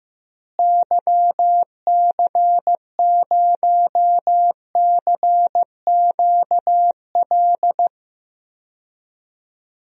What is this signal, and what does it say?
Morse code, heard as a keyed tone, about 16 wpm, 700 Hz. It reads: YC0CQL
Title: DXCC-100 Call Signs - 6 Characters - Repeated 3x - 15wpm